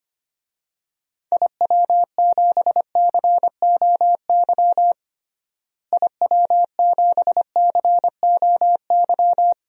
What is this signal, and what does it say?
Morse code, heard as a keyed tone, about 25 wpm, 700 Hz. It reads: IW7COY IW7COY
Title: DXCC-100 Call Signs - 6 Characters - Repeated 2x - 25wpm